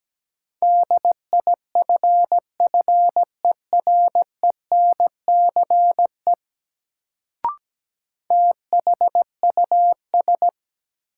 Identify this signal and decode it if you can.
Morse code, heard as a keyed tone, about 17 wpm, 700 Hz. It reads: DIFFERENCE THUS
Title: Sentences from Top 800 Words - New Word Drill - 17wpm